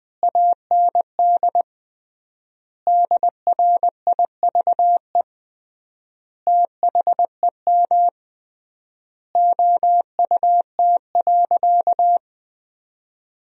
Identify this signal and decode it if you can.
Morse code, heard as a keyed tone, about 21 wpm, 700 Hz. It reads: AND DRIVE THEM OUT.
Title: 3x Word-Spacing - Sentences from Top 500 Words - 20wpm